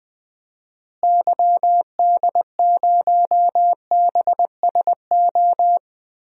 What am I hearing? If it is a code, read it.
Morse code, heard as a keyed tone, about 20 wpm, 700 Hz. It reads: YD0BSO
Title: DXCC-100 Call Signs - 6 Characters - Repeated 3x - 20wpm